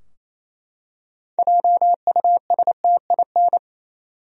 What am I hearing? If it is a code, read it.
Morse code, heard as a keyed tone, about 28 wpm, 700 Hz. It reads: JUSTIN
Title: USA Common Names - 28wpm